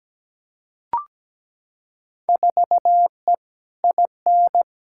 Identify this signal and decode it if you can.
Morse code, heard as a keyed tone, about 17 wpm, 700 Hz. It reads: 4E IN
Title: Field Day Exchanges - 17wpm